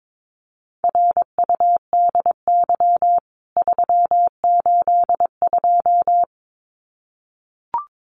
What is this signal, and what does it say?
Morse code, heard as a keyed tone, about 22 wpm, 700 Hz. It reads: RUDY 382
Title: CWT Exchange 22wpm